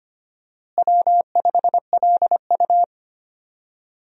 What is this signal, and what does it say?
Morse code, heard as a keyed tone, about 25 wpm, 700 Hz. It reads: W5LU
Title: USA Call Signs - 4 Characters - Repeated 3x - 25wpm